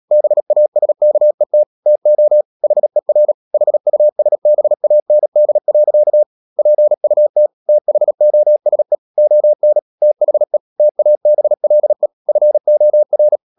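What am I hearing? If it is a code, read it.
Morse code, heard as a keyed tone, about 37 wpm, 600 Hz. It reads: BASKET TO HER HUSBAND. PUT THOSE ON THE TABLE FOR
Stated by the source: Morse code practice at 35–39 wpm sending English text of Ken Follett's novel Never, from CW med Gnister.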